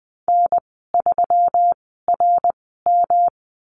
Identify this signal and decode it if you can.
Morse code, heard as a keyed tone, about 20 wpm, spaced as for 14 wpm, 700 Hz. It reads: N3RM
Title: 154 Lesson 25g - Letter p - Call Signs 20wpm